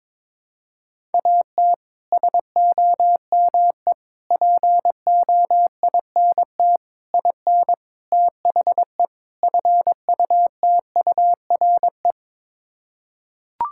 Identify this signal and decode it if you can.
Morse code, heard as a keyed tone, about 22 wpm, 700 Hz. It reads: AT SOME POINT IN THE FUTURE
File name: practice / 22wpm / Sets of 6 Words - 22wpm